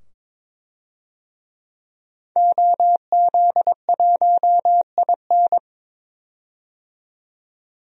Morse code, heard as a keyed tone, about 22 wpm, 700 Hz. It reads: OZ1IN
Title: DXCC-100 Call Signs - 5 Characters - 22wpm